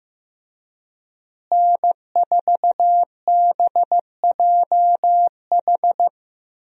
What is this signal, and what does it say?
Morse code, heard as a keyed tone, about 15 wpm, 700 Hz. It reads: N4BJH
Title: USA Call Signs - 5 Characters - Repeated 3x - 15wpm